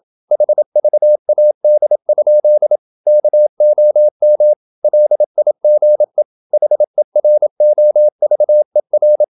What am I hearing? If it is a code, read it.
Morse code, heard as a keyed tone, about 27 wpm, 600 Hz. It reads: HVAD? KOM LIGE HEROVER
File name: Ferie QRQ 27wpm